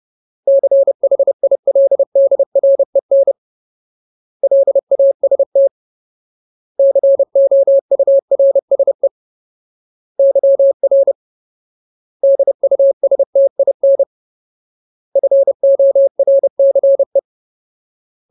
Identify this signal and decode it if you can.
Morse code, heard as a keyed tone, about 30 wpm, 550 Hz. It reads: CHILDREN LAST COURSE YR DUSTIN FORCE
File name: Ultimate Word List / 30WPM Ultimate